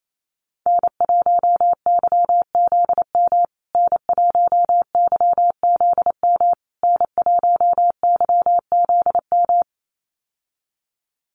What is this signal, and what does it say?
Morse code, heard as a keyed tone, about 28 wpm, 690 Hz. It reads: N1YZM N1YZM N1YZM
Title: USA Call Signs Repeated 3x 28wpm